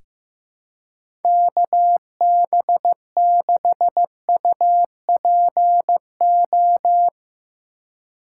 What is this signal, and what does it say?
Morse code, heard as a keyed tone, about 16 wpm, 700 Hz. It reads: KB6UPO
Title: USA Call Signs - 6 Characters - 2x Character Spacing - 15wpm